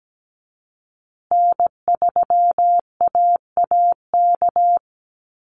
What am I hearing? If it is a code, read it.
Morse code, heard as a keyed tone, about 17 wpm, 700 Hz. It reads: N3AAK
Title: Call Signs Repeated 3x - 17wpm